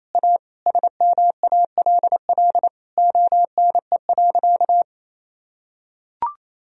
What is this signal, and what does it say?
Morse code, heard as a keyed tone, about 28 wpm, 700 Hz. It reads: A SMALL ONE.
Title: Sentences from Top 1000 Words - Review Part 1 - 28wpm